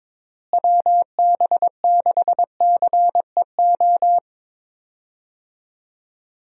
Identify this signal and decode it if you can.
Morse code, heard as a keyed tone, about 22 wpm, 700 Hz. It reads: WB6CEO
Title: USA Call Signs - 6 Characters - Repeated 3x - 22wpm